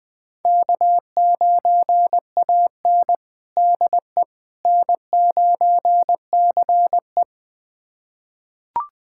Morse code, heard as a keyed tone, about 21 wpm, 700 Hz. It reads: K9AN DE N9CE
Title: USA Callsign DE Callsign - 4 Characters - 20wpm